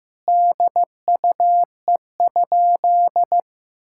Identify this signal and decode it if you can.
Morse code, heard as a keyed tone, about 15 wpm, 700 Hz. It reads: DUE?
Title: Sentences from Top 1000 Words - Review Part 6 - 15wpm